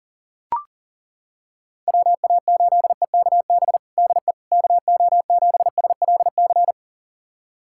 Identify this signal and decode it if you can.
Morse code, heard as a keyed tone, about 40 wpm, 700 Hz. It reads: WA8EKB DE KO7SLC
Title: USA Callsign DE Callsign - 6 Characters - 40wpm